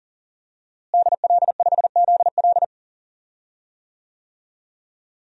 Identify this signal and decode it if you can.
Morse code, heard as a keyed tone, about 40 wpm, 700 Hz. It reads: DL5ZL
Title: DXCC-100 Call Signs - 5 Characters - Repeated 3x - 40wpm